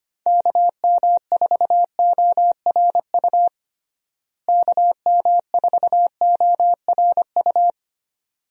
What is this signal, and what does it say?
Morse code, heard as a keyed tone, about 25 wpm, 700 Hz. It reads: KM4ORU KM4ORU
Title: USA Call Signs - 6 Characters - Repeated 2x - 25wpm